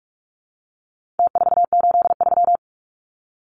Morse code, heard as a keyed tone, about 44 wpm, 700 Hz. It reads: T483
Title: Numbers - Contest Style - 4-Digit 45wpm